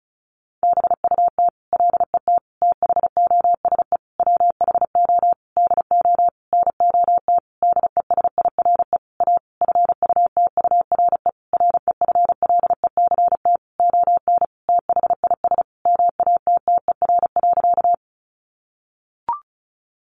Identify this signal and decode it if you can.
Morse code, heard as a keyed tone, about 34 wpm, 700 Hz. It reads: BUT LET THOSE WHO DO NOT DESIRE A FUTURE REFLECT ON THIS MATTER.
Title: Sentences from Top 1500 Words 35wpm